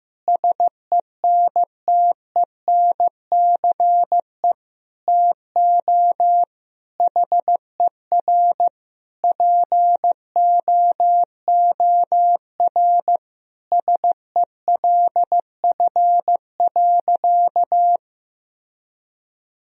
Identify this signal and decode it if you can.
Morse code, heard as a keyed tone, about 15 wpm, 700 Hz. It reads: SENTENCE TO HER POOR SELF.
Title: Sentences from Top 1000 Words - Review Part 1 - 15wpm